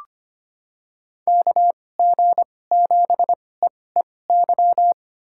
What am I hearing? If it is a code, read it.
Morse code, heard as a keyed tone, about 25 wpm, spaced as for 18 wpm, 700 Hz. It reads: KG7EEY
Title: USA Call Signs - 6 Characters - 2x Character Spacing - 25wpm